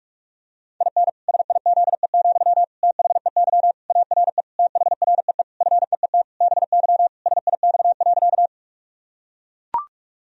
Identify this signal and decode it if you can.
Morse code, heard as a keyed tone, about 41 wpm, 700 Hz. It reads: IN SIZE, THEY ARE THREE FEET BY SIX.
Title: Sentences from Top 1000 Words - Review Part 2 - 45wpm